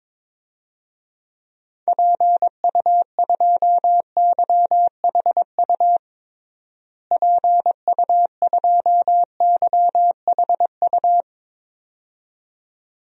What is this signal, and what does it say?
Morse code, heard as a keyed tone, about 22 wpm, 700 Hz. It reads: PU2YHU PU2YHU
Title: DXCC-100 Call Signs - 6 Characters - Repeated 2x - 22wpm